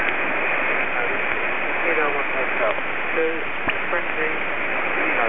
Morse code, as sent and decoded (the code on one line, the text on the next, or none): none